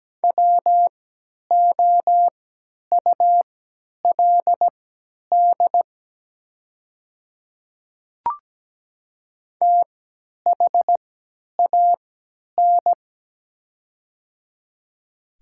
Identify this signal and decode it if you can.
Morse code, heard as a keyed tone, about 17 wpm, spaced as for 10 wpm, 700 Hz. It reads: WOULD THAN
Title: 3x Character Spacing - Top 100 Words - 17wpm